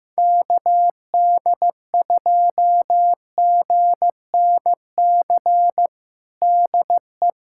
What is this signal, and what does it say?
Morse code, heard as a keyed tone, about 15 wpm, 700 Hz. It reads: KD2GNC DE
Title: USA Callsign DE Callsign - 6 Characters - 15wpm